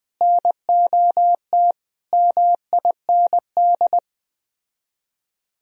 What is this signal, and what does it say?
Morse code, heard as a keyed tone, about 20 wpm, 700 Hz. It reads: NOT MIND
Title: Speed-Racing - Sets of 5 Words 17wpm